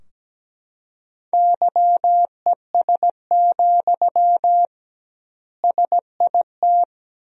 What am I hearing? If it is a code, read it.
Morse code, heard as a keyed tone, about 17 wpm, 700 Hz. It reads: YES, SIT
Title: 2x Word-Spacing - Sentences from Top 500 Words - 17wpm